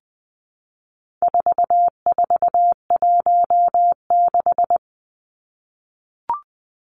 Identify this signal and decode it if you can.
Morse code, heard as a keyed tone, about 20 wpm, 700 Hz. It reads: 4416
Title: Numbers - 4-Digit 20wpm